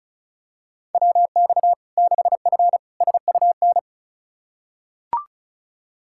Morse code, heard as a keyed tone, about 36 wpm, 700 Hz. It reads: WX 6F SUN
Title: Practice QSO - Exchange 2 - Info - 35wpm